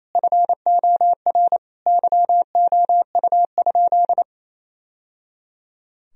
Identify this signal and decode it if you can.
Morse code, heard as a keyed tone, about 28 wpm, 700 Hz. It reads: FOR YOU?
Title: Sentences from Top 1000 Words - Review Part 3 - 28wpm